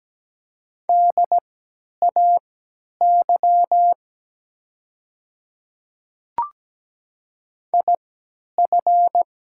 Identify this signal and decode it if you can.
Morse code, heard as a keyed tone, about 17 wpm, spaced as for 10 wpm, 700 Hz. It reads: DAY IF
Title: 3x Character Spacing - Top 300 Words - 17wpm